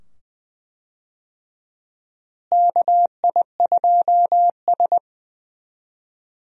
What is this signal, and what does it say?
Morse code, heard as a keyed tone, about 20 wpm, 700 Hz. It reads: KI2S
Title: USA Call Signs - 4 Characters - Repeated 2x - 20wpm